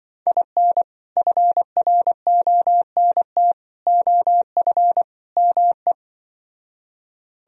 Morse code, heard as a keyed tone, about 24 wpm, 700 Hz. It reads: IN FRONT OF ME
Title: Speed-Racing - Sets of 4 Words 20wpm